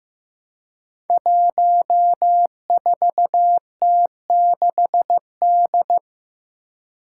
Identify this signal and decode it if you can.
Morse code, heard as a keyed tone, about 15 wpm, 700 Hz. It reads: 14T6D
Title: Five Challenging Letter-Number Combinations 15wpm